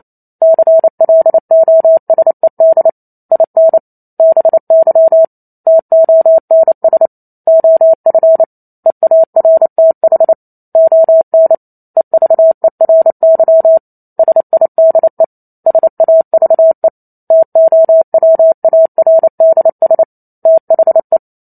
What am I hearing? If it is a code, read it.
Morse code, heard as a keyed tone, about 28 wpm, 650 Hz. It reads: CLOSED IN BY TONS OF EARTH ON EVERY SIDE SAVE TOWARDS THE
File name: Chapter 2 - 28 WPM